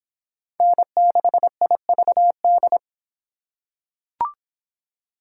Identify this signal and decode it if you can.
Morse code, heard as a keyed tone, about 24 wpm, 700 Hz. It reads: N6IVD